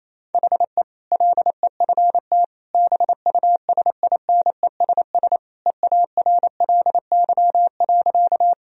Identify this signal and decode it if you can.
Morse code, heard as a keyed tone, about 27 wpm, 690 Hz. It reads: HE LEFT BUSINESS EARLY.
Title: Sentences from Top 1000 Words - Review Part 2 - 28wpm